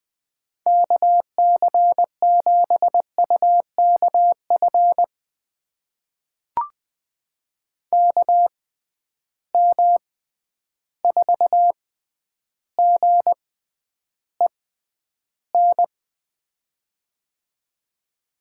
Morse code, heard as a keyed tone, about 20 wpm, spaced as for 10 wpm, 700 Hz. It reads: KC7UKF KM4GEN
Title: USA Call Signs - 6 Characters - 6x Character Spacing - 20wpm